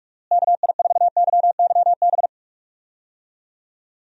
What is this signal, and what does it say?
Morse code, heard as a keyed tone, about 46 wpm, 700 Hz. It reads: KI4YYB
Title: USA Call Signs - 6 Characters - Repeated 3x - 45wpm